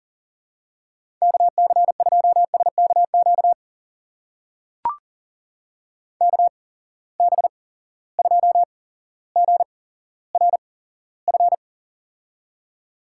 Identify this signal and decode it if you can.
Morse code, heard as a keyed tone, about 40 wpm, spaced as for 17 wpm, 700 Hz. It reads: KC2SKQ KB2GRF
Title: USA Call Signs - 6 Characters - 8x Character Spacing - 40wpm